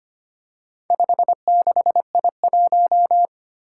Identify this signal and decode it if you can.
Morse code, heard as a keyed tone, about 25 wpm, 700 Hz. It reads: 56I1
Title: Four Challenging Letter-Number Combinations 25wpm